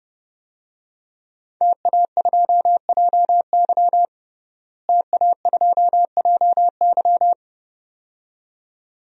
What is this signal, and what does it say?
Morse code, heard as a keyed tone, about 30 wpm, 700 Hz. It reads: TA2JY TA2JY
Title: DXCC-100 Call Signs - 5 Characters - Repeated 2x - 30wpm